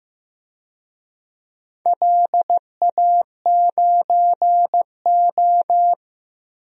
Morse code, heard as a keyed tone, about 15 wpm, 700 Hz. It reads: LA9O
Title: DXCC-100 Call Signs - 4 Characters - 15wpm